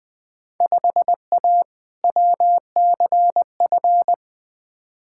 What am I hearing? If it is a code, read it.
Morse code, heard as a keyed tone, about 20 wpm, 700 Hz. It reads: 5A WCF
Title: Field Day Exchanges Encore - 20wpm